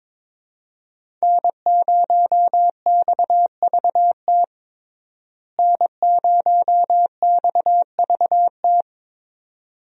Morse code, heard as a keyed tone, about 22 wpm, 700 Hz. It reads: N0XVT N0XVT
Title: USA Call Signs - 5 Characters - Repeated 2x - 22wpm